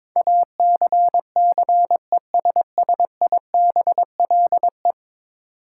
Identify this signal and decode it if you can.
Morse code, heard as a keyed tone, about 22 wpm, 700 Hz. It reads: ACCESSIBLE
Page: morse-code-ninja-practice-22wpm